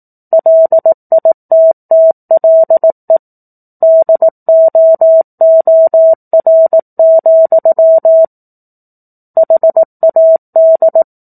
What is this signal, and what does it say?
Morse code, heard as a keyed tone, about 18 wpm, 650 Hz. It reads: LITTLE DOOR, HAD
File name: 04-Chapter-04 18 WPM